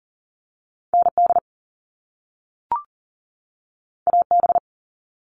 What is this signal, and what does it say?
Morse code, heard as a keyed tone, about 42 wpm, 700 Hz. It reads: ND AB
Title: All Cut Numbers - 2-Digit 40wpm